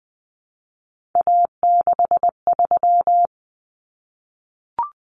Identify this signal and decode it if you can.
Morse code, heard as a keyed tone, about 20 wpm, 700 Hz. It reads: A63
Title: Common Cut Numbers - 3-Digit 20wpm